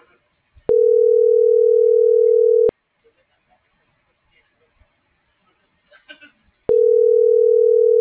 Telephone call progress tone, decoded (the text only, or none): ringback tone